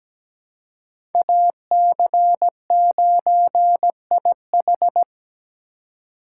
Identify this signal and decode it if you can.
Morse code, heard as a keyed tone, about 17 wpm, 700 Hz. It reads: AC9IH
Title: USA Call Signs - 5 Characters - 2x Character Spacing - 17wpm